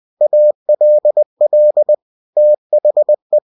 Morse code, heard as a keyed tone, about 20 wpm, 600 Hz. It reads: ALL THE
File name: cw-books / winnie-the-pooh / winnie-20wpm-page-0041